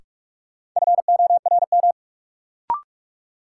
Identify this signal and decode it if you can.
Morse code, heard as a keyed tone, about 44 wpm, 700 Hz. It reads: FORM
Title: Sets of 4 Words - New Word Drill - 45wpm